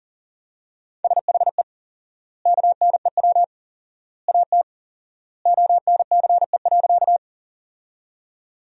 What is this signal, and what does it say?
Morse code, heard as a keyed tone, about 40 wpm, 700 Hz. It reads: SHE KNEW AT ONCE.
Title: 4x Word-Spacing - Sentences from Top 400 Words - 40wpm